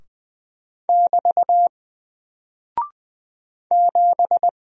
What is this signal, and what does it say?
Morse code, heard as a keyed tone, about 20 wpm, 700 Hz. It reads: <BT> 7
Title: Single Character (Letter:Number:Punctuation:Prosign) Alt 20wpm